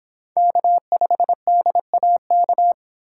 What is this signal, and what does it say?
Morse code, heard as a keyed tone, about 26 wpm, 700 Hz. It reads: K5DAK